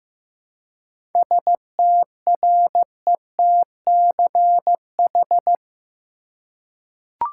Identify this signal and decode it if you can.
Morse code, heard as a keyed tone, about 15 wpm, 700 Hz. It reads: STRETCH
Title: Seven Letter Words - 15wpm